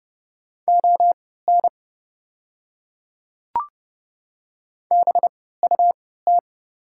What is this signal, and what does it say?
Morse code, heard as a keyed tone, about 30 wpm, spaced as for 17 wpm, 700 Hz. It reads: ON BUT
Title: 3x Character Spacing - Top 300 Words - 30wpm